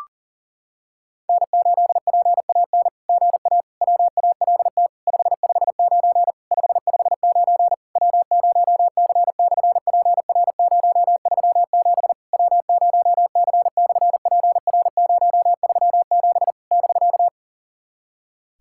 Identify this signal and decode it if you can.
Morse code, heard as a keyed tone, about 40 wpm, 700 Hz. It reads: N8PAN GA WALT 559 559 W0C/PR037 W0C/PR037 <BK>
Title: SOTA QSO Practice 40wpm